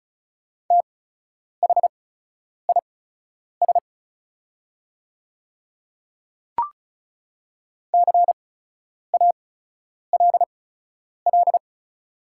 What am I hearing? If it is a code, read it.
Morse code, heard as a keyed tone, about 36 wpm, spaced as for 10 wpm, 700 Hz. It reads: THIS CALL
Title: 8x Character Spacing - Top 100 Words - 35wpm